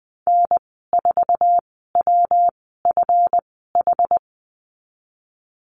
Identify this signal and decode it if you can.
Morse code, heard as a keyed tone, about 20 wpm, spaced as for 14 wpm, 700 Hz. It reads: N4WFH
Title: 161 Lesson 26g - Letter g - Call Signs 20wpm